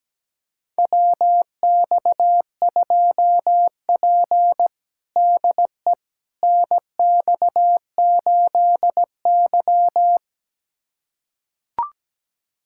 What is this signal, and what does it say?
Morse code, heard as a keyed tone, about 17 wpm, 700 Hz. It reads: WX2P DE NX8Y
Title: USA Callsign DE Callsign - 4 Characters - 17wpm